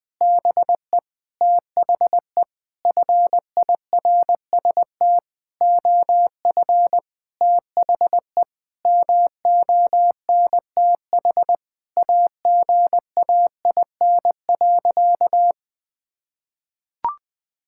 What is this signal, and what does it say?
Morse code, heard as a keyed tone, about 20 wpm, 700 Hz. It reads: BE THE FIRST OF THE MONTH AGAIN.
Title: Sentences from Top 1000 Words - Review Part 5 - 20wpm